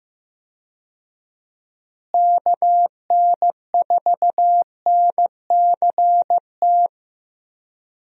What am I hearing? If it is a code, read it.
Morse code, heard as a keyed tone, about 15 wpm, 700 Hz. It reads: KN4NCT
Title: USA Call Signs - 6 Characters - Repeated 2x - 15wpm